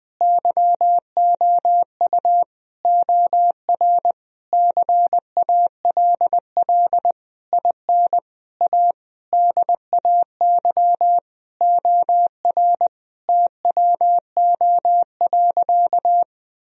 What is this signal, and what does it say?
Morse code, heard as a keyed tone, about 21 wpm, 700 Hz. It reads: YOU OR CALL IN A DAY OR TWO.
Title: Sentences from Top 1000 Words - Review Part 1 - 20wpm